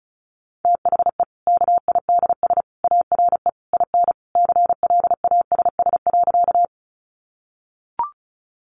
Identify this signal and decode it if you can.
Morse code, heard as a keyed tone, about 34 wpm, 700 Hz. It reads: THE KIDS ARE IN CLASS.
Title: Easy Phrases 35wpm